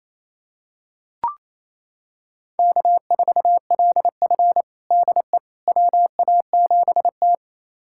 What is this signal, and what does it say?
Morse code, heard as a keyed tone, about 27 wpm, 690 Hz. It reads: K4LF DE WA7T
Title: USA Callsign DE Callsign - 4 Characters - 28wpm